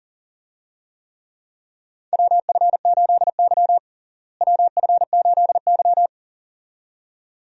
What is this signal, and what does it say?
Morse code, heard as a keyed tone, about 40 wpm, 700 Hz. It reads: WF8Y WF8Y
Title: USA Call Signs - 4 Characters - Repeated 2x - 40wpm